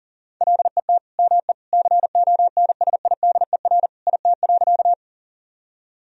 Morse code, heard as a keyed tone, about 40 wpm, 700 Hz. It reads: LET ME CONSIDER IT.
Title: Sentences from Top 1000 Words - Review Part 4 - 40wpm